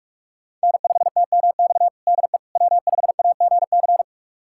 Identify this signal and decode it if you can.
Morse code, heard as a keyed tone, about 46 wpm, 700 Hz. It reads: N5TMX DE W5AGC
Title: USA Callsign DE Callsign - 5 Characters - 45wpm